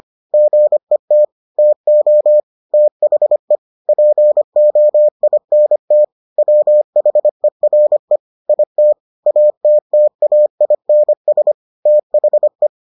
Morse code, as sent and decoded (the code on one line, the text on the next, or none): --. - / - --- / - .... / .--. --- .. -. - / .-- .... .-. / .. - / .- - - .- .. -. ... / - .... .
GET TO THE POINT WHERE IT ATTAINS THE